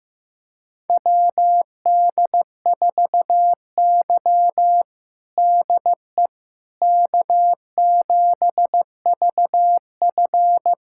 Morse code, heard as a keyed tone, about 16 wpm, 700 Hz. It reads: WD4Y DE K7VF
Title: USA Callsign DE Callsign - 4 Characters - 15wpm